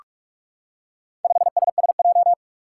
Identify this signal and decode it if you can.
Morse code, heard as a keyed tone, about 44 wpm, 700 Hz. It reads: 5SSJ
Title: Four Challenging Letter-Number Combinations 45wpm